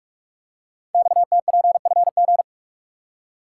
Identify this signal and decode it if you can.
Morse code, heard as a keyed tone, about 44 wpm, 700 Hz. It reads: XTPFG
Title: Code Groups - 5 Letters - 45wpm